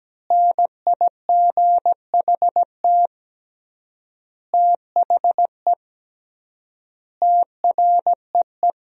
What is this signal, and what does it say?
Morse code, heard as a keyed tone, about 17 wpm, 700 Hz. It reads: NIGHT THE TREE
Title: 3x Word-Spacing - Sentences from Top 500 Words - 17wpm